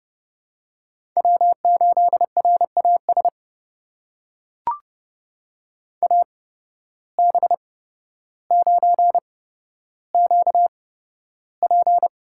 Morse code, heard as a keyed tone, about 30 wpm, spaced as for 13 wpm, 700 Hz. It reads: W8RAS AB9QP
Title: USA Call Signs - 5 Characters - 8x Character Spacing - 30wpm